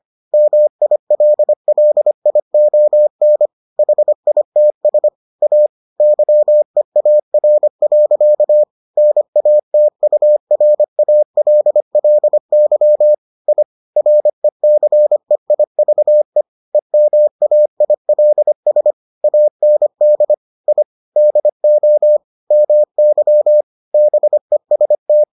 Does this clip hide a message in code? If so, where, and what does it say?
Yes: Morse code from 0.3 to 25.2 s, heard as a keyed tone, about 25 wpm, 600 Hz. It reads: MILLION HITS A YEAR. NATURALLY I RECEIVE EMAILS AND I DO MY BEST